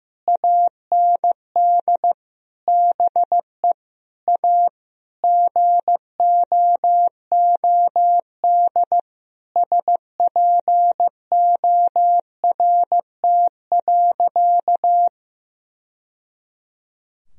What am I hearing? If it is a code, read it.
Morse code, heard as a keyed tone, about 15 wpm, 700 Hz. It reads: AND BE A GOOD SPORT.
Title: Sentences from Top 1000 Words - Review Part 4 - 15wpm